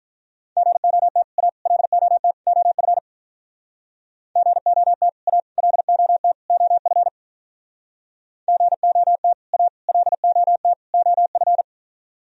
Morse code, heard as a keyed tone, about 45 wpm, 700 Hz. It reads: GOT A LOT OF GOT A LOT OF GOT A LOT OF